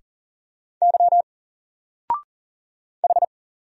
Morse code, heard as a keyed tone, about 40 wpm, 700 Hz. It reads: Y H
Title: Single Letter-Number 40wpm